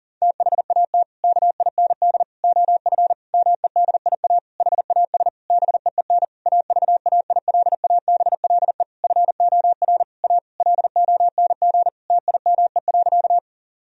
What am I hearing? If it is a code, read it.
Morse code, heard as a keyed tone, about 40 wpm, 700 Hz. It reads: THAT KIND OF MEDIA HAS BEEN AVAILABLE FOR A LONG TIME.
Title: Sentences from Top 1000 Words - Review Part 4 - 40wpm